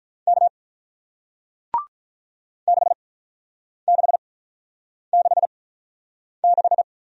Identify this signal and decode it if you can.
Morse code, heard as a keyed tone, about 48 wpm, spaced as for 9 wpm, 700 Hz. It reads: K 6666